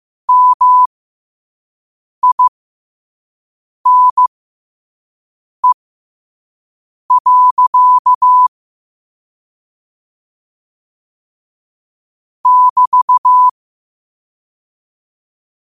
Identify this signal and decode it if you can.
Morse code, heard as a keyed tone, about 15 wpm, spaced as for 5 wpm, 1000 Hz. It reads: MINE. <BT>
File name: qotdfiles / 5wpm